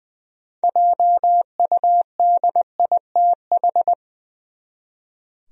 Morse code, heard as a keyed tone, about 20 wpm, 700 Hz. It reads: JUDITH
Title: USA Common Names - 20wpm